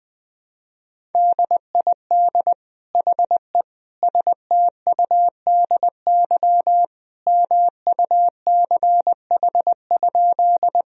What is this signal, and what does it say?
Morse code, heard as a keyed tone, about 20 wpm, 700 Hz. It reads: DID HE STUDY MUCH?
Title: Sentences from Top 1000 Words - Review Part 1 - 20wpm